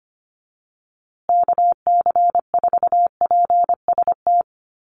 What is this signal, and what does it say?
Morse code, heard as a keyed tone, about 25 wpm, 700 Hz. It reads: KC4PST
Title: USA Call Signs Repeated 3x 25wpm